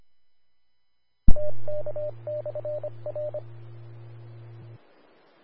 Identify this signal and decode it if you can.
Morse code, heard as a keyed tone, about 25 wpm, 600 Hz. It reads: TK/R
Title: Austin, TX Amateur Repeaters